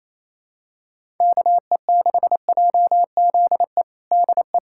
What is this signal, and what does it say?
Morse code, heard as a keyed tone, about 28 wpm, 700 Hz. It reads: KE6JZE DE
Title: USA Callsign DE Callsign - 6 Characters - 28wpm